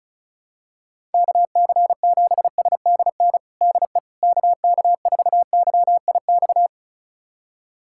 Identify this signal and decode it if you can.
Morse code, heard as a keyed tone, about 35 wpm, 700 Hz. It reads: KC7SDN DE KK4YIX
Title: USA Callsign DE Callsign - 6 Characters - 35wpm